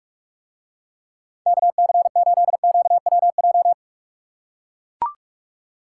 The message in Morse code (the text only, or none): KC8QWJ